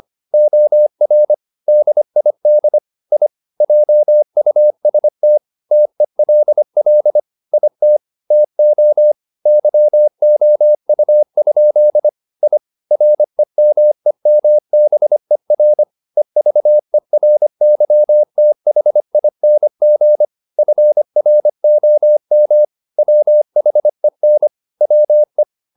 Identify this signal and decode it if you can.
Morse code, heard as a keyed tone, about 25 wpm, 600 Hz. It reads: OR DID I JUST TELL IT TO YOU? I REMEMBER EVERYTHING FROM WHEN WE
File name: the old man and the sea part 1 25wpm